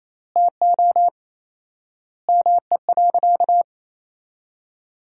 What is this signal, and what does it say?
Morse code, heard as a keyed tone, about 27 wpm, 700 Hz. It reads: TO ME.
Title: 4x Word-Spacing - Sentences from Top 400 Words - 28wpm